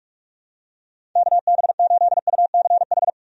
Morse code, heard as a keyed tone, about 44 wpm, 700 Hz. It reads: KB8UCH
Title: USA Call Signs - 6 Characters - 45wpm